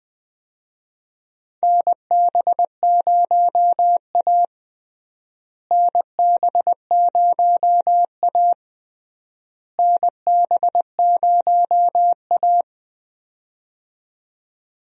Morse code, heard as a keyed tone, about 20 wpm, 690 Hz. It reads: NB0A NB0A NB0A
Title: USA Call Signs - 4 Characters - Repeated 3x - 20wpm